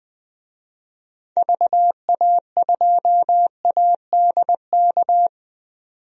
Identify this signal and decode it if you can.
Morse code, heard as a keyed tone, about 21 wpm, 700 Hz. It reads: VA2ADK
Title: DXCC-100 Call Signs - 6 Characters - 20wpm